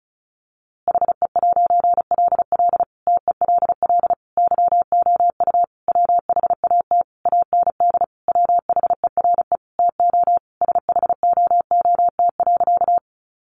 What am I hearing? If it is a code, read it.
Morse code, heard as a keyed tone, about 34 wpm, 700 Hz. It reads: HE'LL TELL YOU WHAT AND WHERE TO SHOOT.
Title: A Sound Of Thunder 35wpm